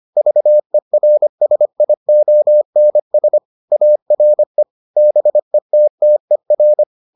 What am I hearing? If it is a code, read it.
Morse code, heard as a keyed tone, about 25 wpm, 600 Hz. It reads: VERSIONS ARE BETTER